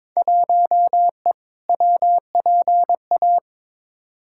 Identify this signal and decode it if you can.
Morse code, heard as a keyed tone, about 22 wpm, 700 Hz. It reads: 1E WPA
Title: Field Day Exchanges - 22wpm